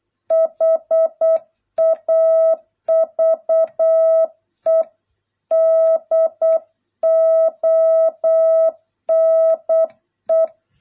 Morse code, heard as a keyed tone, about 8 wpm, 650 Hz. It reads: HAVE DONE